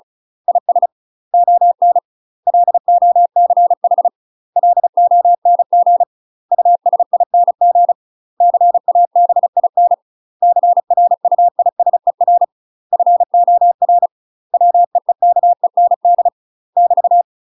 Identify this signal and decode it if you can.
Morse code, heard as a keyed tone, about 34 wpm, 700 Hz. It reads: IS ON LOCH LONG USING CABIN CRUISER FOR WEEKEND <BT>
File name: UK MASTER QSOs TRAINING/35WPM